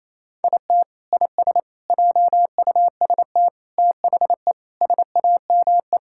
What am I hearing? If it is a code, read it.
Morse code, heard as a keyed tone, about 28 wpm, 700 Hz. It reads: IT IS JUST THE SAME
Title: Sentences from Top 1000 Words - Review Part 6 - 28wpm